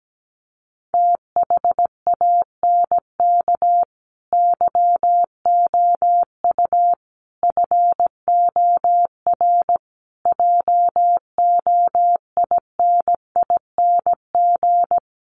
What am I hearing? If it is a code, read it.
Morse code, heard as a keyed tone, about 17 wpm, 700 Hz. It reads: THANK YOU FOR JOINING
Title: Sets of 5 Words 17wpm